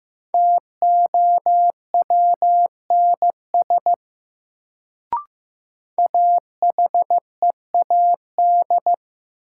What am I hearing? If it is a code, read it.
Morse code, heard as a keyed tone, about 15 wpm, 700 Hz. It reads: TOWNS AHEAD
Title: Sentences from Top 1000 Words - Review Part 6 - New Word Drill - 15wpm